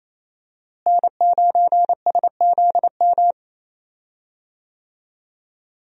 Morse code, heard as a keyed tone, about 28 wpm, 700 Hz. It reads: N9SZM
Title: USA Call Signs - 5 Characters - Repeated 3x - 28wpm